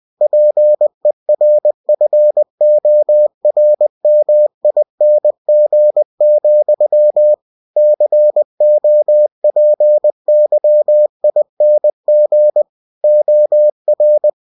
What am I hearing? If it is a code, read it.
Morse code, heard as a keyed tone, about 20 wpm, 600 Hz. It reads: PERFORMING, COPYING OR
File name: cw-books / winnie-the-pooh / winnie-20wpm-page-0116